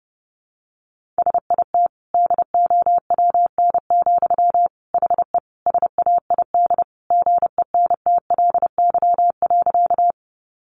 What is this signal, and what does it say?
Morse code, heard as a keyed tone, about 30 wpm, 700 Hz. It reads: SIT DOWN, HE SAID GENTLY.
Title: Sentences from Top 1500 Words 30wpm